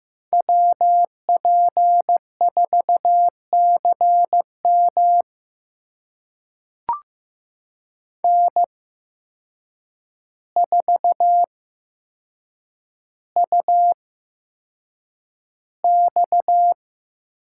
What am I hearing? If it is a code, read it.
Morse code, heard as a keyed tone, about 15 wpm, spaced as for 7 wpm, 700 Hz. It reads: WP4CM N4UX
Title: USA Call Signs - 5 Characters - 8x Character Spacing - 15wpm